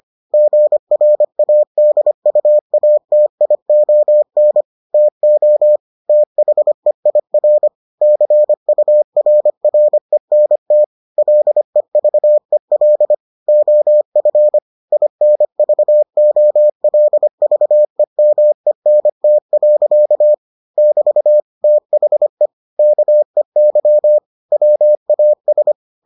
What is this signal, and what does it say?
Morse code, heard as a keyed tone, about 25 wpm, 600 Hz. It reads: GRADUATION TO THEIR CURRENT LEVEL OF INVOLVEMENT. <BT> THE KEY WAS